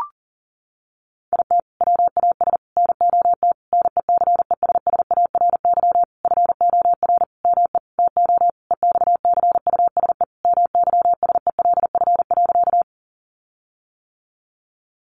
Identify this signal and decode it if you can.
Morse code, heard as a keyed tone, about 37 wpm, 700 Hz. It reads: IT WAS NOT NECESSARY FOR ME TO EXCUSE MYSELF.
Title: Sentences from Top 1300 Words 40wpm